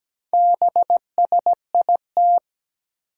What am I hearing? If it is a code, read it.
Morse code, heard as a keyed tone, about 17 wpm, 700 Hz. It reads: BSIT
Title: Four Challenging Letter-Number Combinations 17wpm